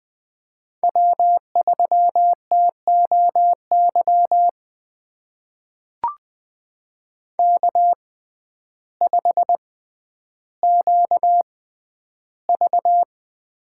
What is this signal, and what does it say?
Morse code, heard as a keyed tone, about 20 wpm, spaced as for 11 wpm, 700 Hz. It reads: W3TOY K5QV
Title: USA Call Signs - 6x Character Spacing - 20wpm